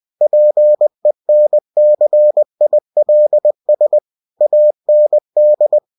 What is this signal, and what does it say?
Morse code, heard as a keyed tone, about 20 wpm, 600 Hz. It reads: PENCILS AND
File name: cw-books / winnie-the-pooh / winnie-20wpm-page-0109